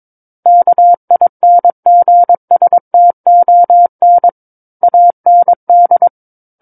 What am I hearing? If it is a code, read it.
Morse code, heard as a keyed tone, about 22 wpm, 700 Hz. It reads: KINGSTON AND
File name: Book One / 13-Chapter 13 - 22 WPM